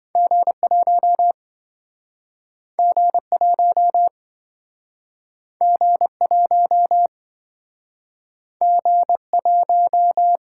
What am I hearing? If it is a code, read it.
Morse code, heard as a keyed tone, about 27 wpm, 700 Hz. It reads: G1 G1 G1 G1